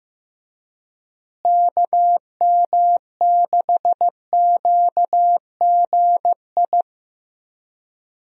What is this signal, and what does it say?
Morse code, heard as a keyed tone, about 15 wpm, 700 Hz. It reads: KM6QGI
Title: USA Call Signs - 6 Characters - Repeated 2x - 15wpm